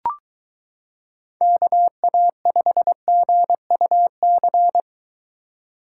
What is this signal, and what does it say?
Morse code, heard as a keyed tone, about 22 wpm, 700 Hz. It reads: KA5GUC